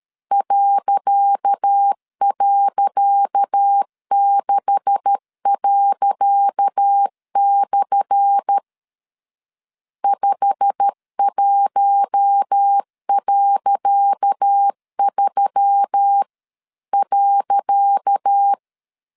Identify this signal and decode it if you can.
Morse code, heard as a keyed tone, about 13 wpm, 800 Hz. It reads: ..6./ 51.3.
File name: LEZIONI / 10 SIGNS / 10C DOT 12wpm 2451